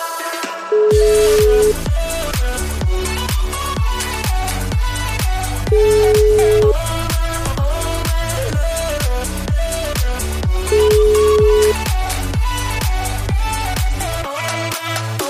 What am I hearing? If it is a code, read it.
Ringback tone (Europe, 425 Hz), with noise.